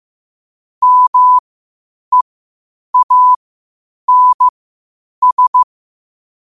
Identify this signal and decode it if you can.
Morse code, heard as a keyed tone, about 15 wpm, spaced as for 8 wpm, 1000 Hz. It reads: MEANS
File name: qotdfiles / 8wpm